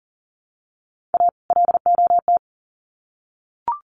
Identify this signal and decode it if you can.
Morse code, heard as a keyed tone, about 39 wpm, 700 Hz. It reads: A LOT
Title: Sets of 2 Words 40wpm